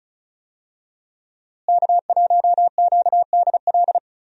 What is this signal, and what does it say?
Morse code, heard as a keyed tone, about 34 wpm, 700 Hz. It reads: K1QDL
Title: USA Call Signs - 5 Characters - Repeated 3x - 35wpm